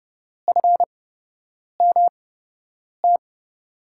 Morse code, heard as a keyed tone, about 31 wpm, spaced as for 8 wpm, 700 Hz. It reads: FMT
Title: USA Call Signs - 6 Characters - 8x Character Spacing - 30wpm